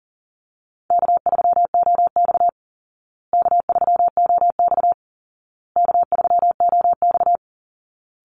Morse code, heard as a keyed tone, about 40 wpm, 700 Hz. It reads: K3OX K3OX K3OX
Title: Call Signs Repeated 2x - 40wpm